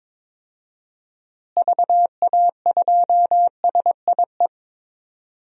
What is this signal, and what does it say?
Morse code, heard as a keyed tone, about 22 wpm, 700 Hz. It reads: VA2SIE
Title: DXCC-100 Call Signs - 6 Characters - Repeated 3x - 22wpm